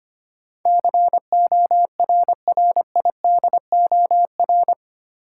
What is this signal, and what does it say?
Morse code, heard as a keyed tone, about 25 wpm, 700 Hz. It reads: CORRIDOR
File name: Eight Letter Words - 25wpm